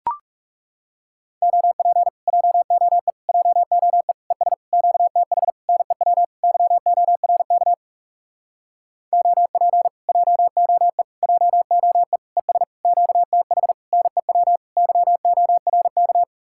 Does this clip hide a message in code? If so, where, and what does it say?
Yes: Morse code from 1.4 to 16.2 s, heard as a keyed tone, about 40 wpm, 700 Hz. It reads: OP JOE JOE ES QTH NEW YORK OP JOE JOE ES QTH NEW YORK